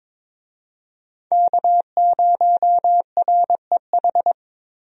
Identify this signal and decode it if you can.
Morse code, heard as a keyed tone, about 21 wpm, 700 Hz. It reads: K0REH
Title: USA Call Signs - 5 Characters - Repeated 2x - 22wpm